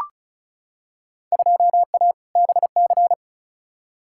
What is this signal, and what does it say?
Morse code, heard as a keyed tone, about 34 wpm, 700 Hz. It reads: 2A BC
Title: Field Day Exchanges Encore - 35wpm